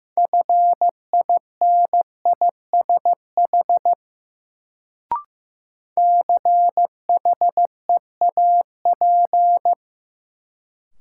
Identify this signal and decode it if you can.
Morse code, heard as a keyed tone, about 15 wpm, 700 Hz. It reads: FINISH CHEAP
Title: Sentences from Top 1200 Words - New Word Drill Alt - 15wpm